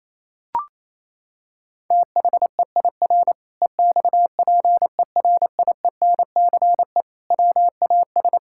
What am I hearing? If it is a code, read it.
Morse code, heard as a keyed tone, about 28 wpm, 700 Hz. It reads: THEIR EXPERIENCE WAS
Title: Sentences from Top 1000 Words - Review Part 5 - 28wpm